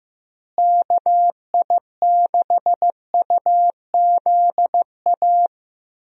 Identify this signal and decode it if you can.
Morse code, heard as a keyed tone, about 15 wpm, 700 Hz. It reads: KI6UZA
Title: USA Call Signs - 6 Characters - Repeated 2x - 15wpm